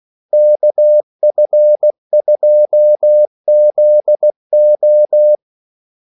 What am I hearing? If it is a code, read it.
Morse code, heard as a keyed tone, about 16 wpm, 600 Hz. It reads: KF2ZO
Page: Kaldesignaler 16 wpm